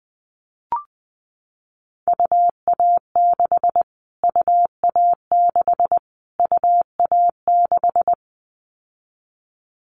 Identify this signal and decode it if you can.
Morse code, heard as a keyed tone, about 20 wpm, 700 Hz. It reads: UA6 UA6 UA6
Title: Callsign Prefixes for Top 25 DXCC - Repeated 3x - 20wpm